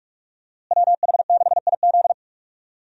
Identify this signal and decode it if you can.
Morse code, heard as a keyed tone, about 46 wpm, 700 Hz. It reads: WH6IZ
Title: USA Call Signs - 5 Characters - Repeated 2x - 45wpm